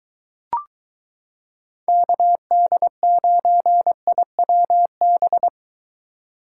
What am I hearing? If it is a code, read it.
Morse code, heard as a keyed tone, about 22 wpm, 700 Hz. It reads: KD9IWB